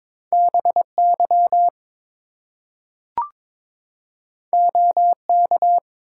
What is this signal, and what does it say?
Morse code, heard as a keyed tone, about 22 wpm, 700 Hz. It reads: BY OK
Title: Two Letter Words - 22wpm